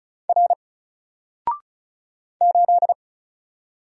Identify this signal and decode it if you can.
Morse code, heard as a keyed tone, about 36 wpm, 700 Hz. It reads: R 8
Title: Single Letter-Number Alt 35wpm